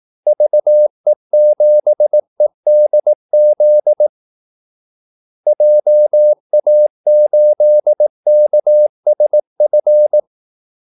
Morse code, heard as a keyed tone, about 18 wpm, 600 Hz. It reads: VE7EDZ JA8KSF
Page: Kaldesignaler 18 wpm